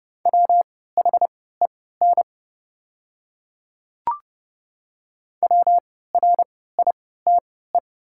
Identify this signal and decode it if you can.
Morse code, heard as a keyed tone, about 30 wpm, spaced as for 17 wpm, 700 Hz. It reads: WHEN WRITE
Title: 3x Character Spacing - Top 100 Words - 30wpm